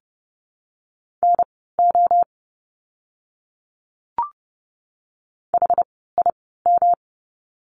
Morse code, heard as a keyed tone, about 30 wpm, spaced as for 17 wpm, 700 Hz. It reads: NO HIM
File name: course / 30:10wpm / 237 Lesson 38f - Letter q - Words 30wpm